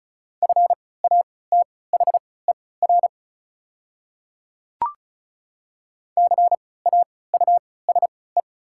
Morse code, heard as a keyed tone, about 36 wpm, spaced as for 20 wpm, 690 Hz. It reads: FATHER CAUSE
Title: 3x Character Spacing - Top 200 Words - 35wpm